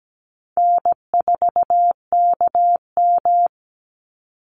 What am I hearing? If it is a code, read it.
Morse code, heard as a keyed tone, about 17 wpm, 700 Hz. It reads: N4KM
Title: Easy Call Signs Repeated 2x - 17wpm